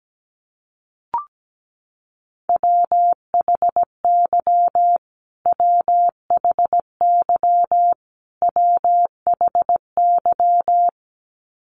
Morse code, heard as a keyed tone, about 17 wpm, 700 Hz. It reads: WHY WHY WHY
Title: Top 200 Words Repeated 17wpm